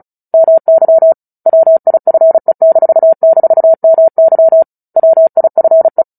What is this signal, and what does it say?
Morse code, heard as a keyed tone, about 34 wpm, 650 Hz. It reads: MY WIFE--MY WIFE
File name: Book Two / 09-Chapter 9 - 35 WPM